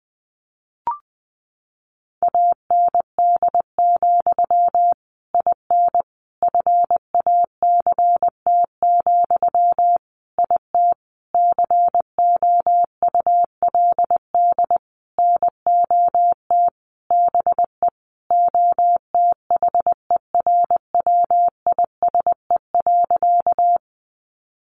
Morse code, heard as a keyed tone, about 20 wpm, 700 Hz. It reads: AND, IN FACT, IT COULD NOT BE OTHERWISE.
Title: Sentences from Top 1200 Words 20wpm